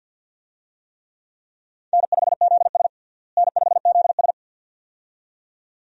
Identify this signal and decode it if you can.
Morse code, heard as a keyed tone, about 51 wpm, 700 Hz. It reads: N5ZS N5ZS
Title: USA Call Signs - 4 Characters - Repeated 2x - 50wpm